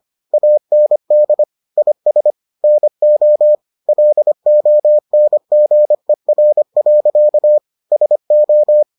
Morse code, heard as a keyed tone, about 25 wpm, 600 Hz. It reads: AND IS NO LONGER. SO